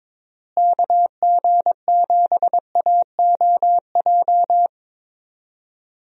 Morse code, heard as a keyed tone, about 22 wpm, 700 Hz. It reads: KG7AOJ
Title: USA Call Signs - 6 Characters - Repeated 3x - 22wpm